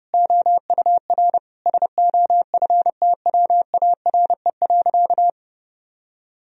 Morse code, heard as a keyed tone, about 32 wpm, 700 Hz. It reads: OUR SOFTWARE.
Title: Sentences from Top 1000 Words - Review Part 3 - 30wpm